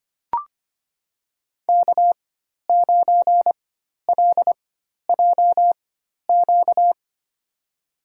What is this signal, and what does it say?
Morse code, heard as a keyed tone, about 25 wpm, spaced as for 12 wpm, 700 Hz. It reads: K9LJQ
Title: USA Call Signs - 5 Characters - 4x Character Spacing - 25wpm